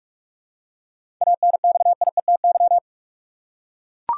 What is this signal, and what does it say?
Morse code, heard as a keyed tone, about 42 wpm, 700 Hz. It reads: ANXIETY
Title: Seven Letter Words - 45wpm